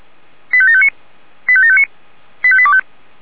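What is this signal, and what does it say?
Five-tone selective call (ZVEI), heard as quick runs of five tones, tones of about 70 ms. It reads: 85759 85759 85825
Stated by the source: ZVEI five-tone selective call received on an SDR.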